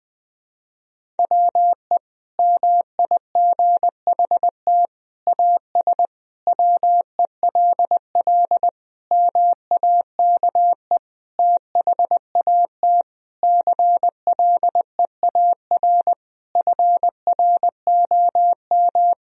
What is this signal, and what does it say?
Morse code, heard as a keyed tone, about 20 wpm, 700 Hz. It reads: WE MIGHT AS WELL MAKE THAT CLEAR FROM
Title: Sentences from Top 1000 Words - Review Part 1 - 20wpm